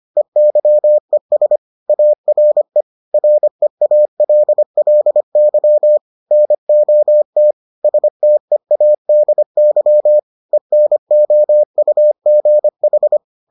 Morse code, heard as a keyed tone, about 25 wpm, 600 Hz. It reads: EYES ARE REALLY NOT STEADY ENOUGH